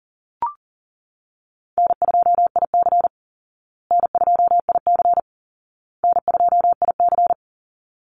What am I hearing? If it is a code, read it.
Morse code, heard as a keyed tone, about 40 wpm, 700 Hz. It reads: N2IC N2IC N2IC
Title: Easy Call Signs Repeated 3x - 40wpm